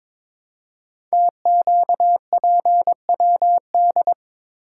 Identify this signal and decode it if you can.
Morse code, heard as a keyed tone, about 22 wpm, 700 Hz. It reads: TQPWD
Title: Code Groups - 5 Letters - 22wpm